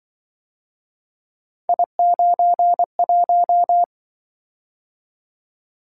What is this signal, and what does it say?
Morse code, heard as a keyed tone, about 24 wpm, 700 Hz. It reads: I91
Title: Speed-Racing - Three Letter-Number Combinations - 20wpm